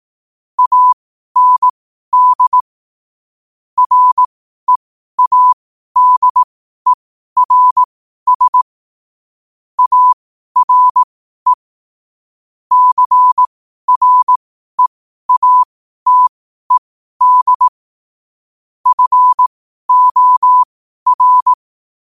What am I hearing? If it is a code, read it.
Morse code, heard as a keyed tone, about 18 wpm, spaced as for 12 wpm, 1000 Hz. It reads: AND READERS ARE CREATED FOR
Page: New quotes every day in morse code at 12 Words per minute.